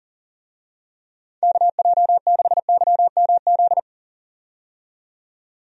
Morse code, heard as a keyed tone, about 40 wpm, 700 Hz. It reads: KJ6YMZ
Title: USA Call Signs - 6 Characters - 40wpm